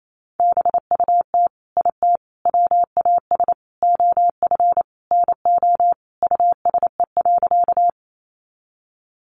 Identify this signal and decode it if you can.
Morse code, heard as a keyed tone, about 28 wpm, 700 Hz. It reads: BUT IT WAS OF NO USE.
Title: Sentences from Top 100 Words 28wpm